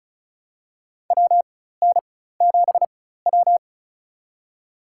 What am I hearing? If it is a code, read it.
Morse code, heard as a keyed tone, about 36 wpm, spaced as for 16 wpm, 700 Hz. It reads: WN7W
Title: USA Call Signs - 4x Character Spacing - 35wpm